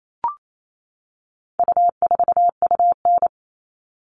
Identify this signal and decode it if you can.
Morse code, heard as a keyed tone, about 28 wpm, 700 Hz. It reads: U4UN
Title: All Cut Numbers - 4-Digit 28wpm